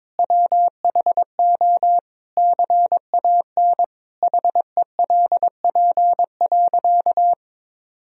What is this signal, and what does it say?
Morse code, heard as a keyed tone, about 22 wpm, 700 Hz. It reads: WHO CAN HELP.
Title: Sentences from Top 1000 Words - Review Part 4 - 22wpm